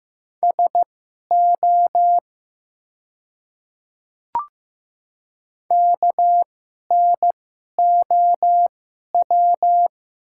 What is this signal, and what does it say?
Morse code, heard as a keyed tone, about 15 wpm, spaced as for 11 wpm, 700 Hz. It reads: SO KNOW
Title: 2x Character Spacing - Top 100 Words - 15wpm